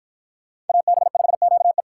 Morse code, heard as a keyed tone, about 53 wpm, 700 Hz. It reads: AB5QE